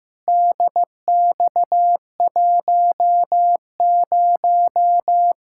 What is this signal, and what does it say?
Morse code, heard as a keyed tone, about 15 wpm, 700 Hz. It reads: DX10
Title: Practice QSO - Exchange 2 - Info - 15wpm